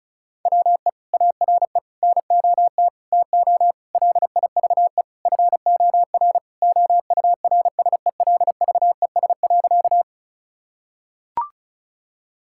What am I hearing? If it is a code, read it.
Morse code, heard as a keyed tone, about 36 wpm, 700 Hz. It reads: WE ARE NOT TO LIVE FOR OURSELVES.
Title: Sentences from Top 1600 Words 35wpm